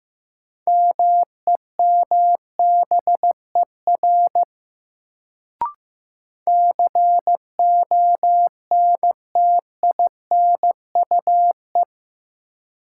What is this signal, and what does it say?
Morse code, heard as a keyed tone, about 15 wpm, 700 Hz. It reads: MEMBER CONTINUE
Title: Sentences from Top 1100 Words - New Word Drill Alt - 15wpm